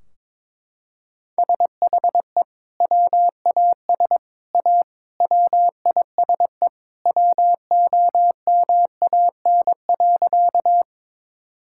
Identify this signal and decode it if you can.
Morse code, heard as a keyed tone, about 22 wpm, 700 Hz. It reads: SHE WAS A WISE WOMAN.
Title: Sentences from Top 1600 Words 22wpm